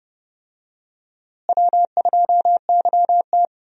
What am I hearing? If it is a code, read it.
Morse code, heard as a keyed tone, about 30 wpm, 700 Hz. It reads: W2YT
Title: USA Call Signs - 4 Characters - Repeated 3x - 30wpm